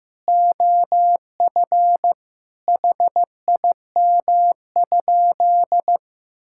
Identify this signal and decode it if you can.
Morse code, heard as a keyed tone, about 15 wpm, 700 Hz. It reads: OF HIM?
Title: Sentences from Top 1000 Words - Review Part 5 - 15wpm